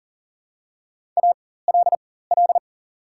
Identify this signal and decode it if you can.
Morse code, heard as a keyed tone, about 40 wpm, spaced as for 19 wpm, 700 Hz. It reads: ALL
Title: 4x Character Spacing - Top 200 Words - 40wpm